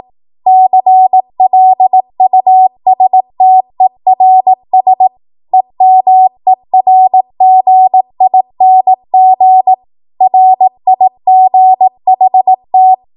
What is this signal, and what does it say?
Morse code, heard as a keyed tone, about 18 wpm, 750 Hz. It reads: CLUSTERS EMERGING RIGHT
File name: Archive / 18 WPM / 18WPM